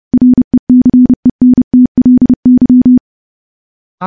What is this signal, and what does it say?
Morse code, heard as a keyed tone, about 30 wpm, 250 Hz. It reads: RECENTLY
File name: morse-training-30WPM